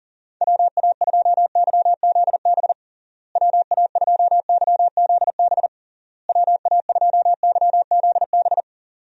Morse code, heard as a keyed tone, about 40 wpm, 700 Hz. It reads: WA2YZB WA2YZB WA2YZB
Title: USA Call Signs - 6 Characters - Repeated 3x - 40wpm